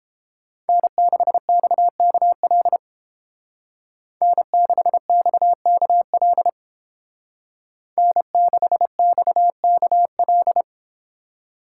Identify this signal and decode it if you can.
Morse code, heard as a keyed tone, about 31 wpm, 700 Hz. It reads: N6XKL N6XKL N6XKL